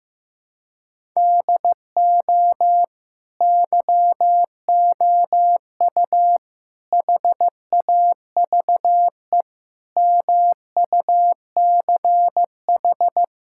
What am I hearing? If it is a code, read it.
Morse code, heard as a keyed tone, about 15 wpm, 700 Hz. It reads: DO YOU HAVE MUCH
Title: Sentences from Top 1000 Words - Review Part 6 - 15wpm